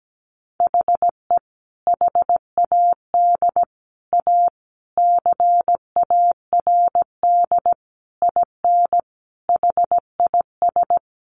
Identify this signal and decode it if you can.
Morse code, heard as a keyed tone, about 17 wpm, 700 Hz. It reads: HE HAD A CARD IN HIS
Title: Sentences from Top 600 Words 17wpm